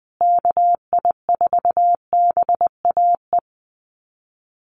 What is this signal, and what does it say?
Morse code, heard as a keyed tone, about 19 wpm, 700 Hz. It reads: KI4BAE
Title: USA Call Signs Repeated 2x 20wpm